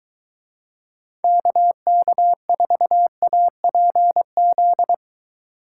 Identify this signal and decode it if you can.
Morse code, heard as a keyed tone, about 23 wpm, 700 Hz. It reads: KK4APZ